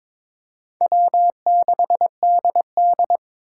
Morse code, heard as a keyed tone, about 26 wpm, 700 Hz. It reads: W6DD
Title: Speed Racing - USA 1x2 Call Signs 22wpm